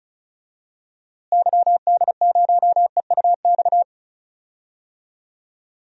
Morse code, heard as a keyed tone, about 34 wpm, 700 Hz. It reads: YD0EUX
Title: DXCC-100 Call Signs - 6 Characters - Repeated 3x - 35wpm